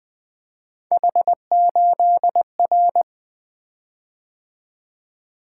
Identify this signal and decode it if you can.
Morse code, heard as a keyed tone, about 20 wpm, 700 Hz. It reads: H8R
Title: Three Letter-Number Combinations - 20wpm